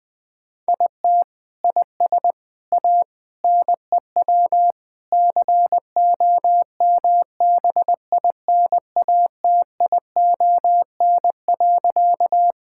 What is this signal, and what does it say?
Morse code, heard as a keyed tone, about 20 wpm, 700 Hz. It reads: IT IS A NEW COMBINATION.
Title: Sentences from Top 1000 Words - Review Part 6 - 20wpm